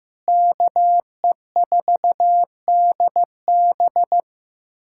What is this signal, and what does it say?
Morse code, heard as a keyed tone, about 15 wpm, 700 Hz. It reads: KE4DB
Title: USA Callsign DE Callsign - 5 Characters - 15wpm